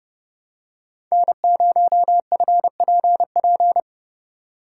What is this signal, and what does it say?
Morse code, heard as a keyed tone, about 30 wpm, 700 Hz. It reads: N0FPP
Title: USA Call Signs - 5 Characters - 4x Character Spacing - 30wpm